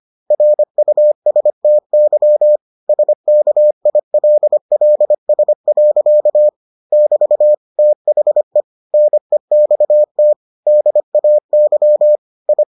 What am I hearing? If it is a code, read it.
Morse code, heard as a keyed tone, about 25 wpm, 600 Hz. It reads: RUSTY SKILLS. <BT> THE NEXT DAY I